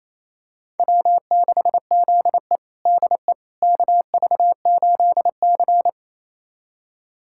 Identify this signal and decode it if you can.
Morse code, heard as a keyed tone, about 28 wpm, 700 Hz. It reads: W6ZE DE KV8C
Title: USA Callsign DE Callsign - 4 Characters - 28wpm